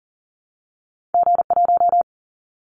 Morse code, heard as a keyed tone, about 40 wpm, 700 Hz. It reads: G1
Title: Callsign Prefixes for Top 25 DXCC - 40wpm